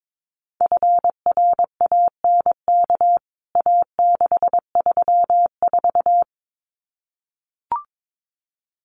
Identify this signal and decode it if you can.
Morse code, heard as a keyed tone, about 22 wpm, 700 Hz. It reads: FRANK A634
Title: CWT Exchange 22wpm